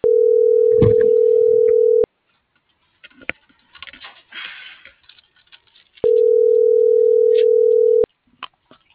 Ringback tone (North America), 440 + 480 Hz.